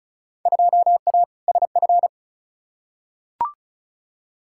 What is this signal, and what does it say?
Morse code, heard as a keyed tone, about 34 wpm, 700 Hz. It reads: 2A SF
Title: Field Day Exchanges - 35wpm